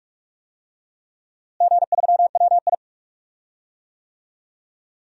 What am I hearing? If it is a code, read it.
Morse code, heard as a keyed tone, about 46 wpm, 700 Hz. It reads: G3WI
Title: DXCC-100 Call Signs - 4 Characters - Repeated 2x - 45wpm